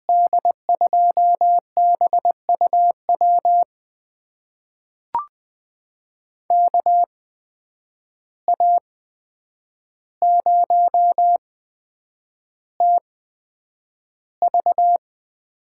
Morse code, heard as a keyed tone, about 21 wpm, spaced as for 9 wpm, 700 Hz. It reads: D2BUW KA0TV
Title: USA Call Signs - 6 Characters - 8x Character Spacing - 20wpm